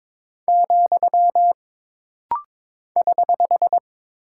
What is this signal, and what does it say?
Morse code, heard as a keyed tone, about 22 wpm, 700 Hz. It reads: , <HH>
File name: practice / 22wpm / Single Punctuation-Prosign - Rapid-Fire - 22wpm